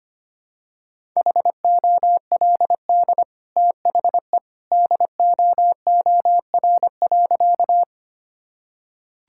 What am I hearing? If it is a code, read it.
Morse code, heard as a keyed tone, about 25 wpm, 700 Hz. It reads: HOLD THE DOOR.
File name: Easy Sentences with Hints - 25wpm